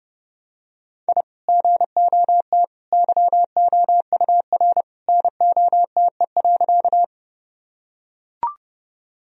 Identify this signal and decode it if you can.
Morse code, heard as a keyed tone, about 32 wpm, 700 Hz. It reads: I GOT YOUR NOTE.
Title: Sentences from Top 1000 Words - Review Part 2 - 30wpm